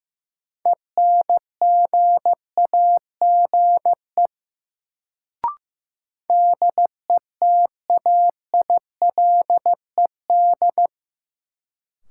Morse code, heard as a keyed tone, about 16 wpm, 700 Hz. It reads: ENGAGE DETAILED
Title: Sentences from Top 1500 Words - New Word Drill - 15wpm